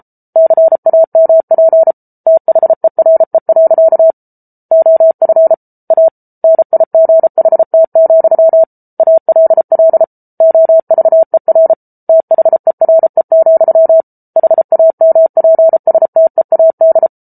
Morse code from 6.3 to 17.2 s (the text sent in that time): NIGHT, ALL OVER THERE, HAMPSTEAD